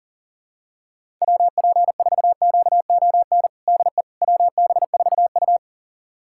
Morse code, heard as a keyed tone, about 37 wpm, 700 Hz. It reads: WP4QON DE WB4U
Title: USA Callsign DE Callsign 40wpm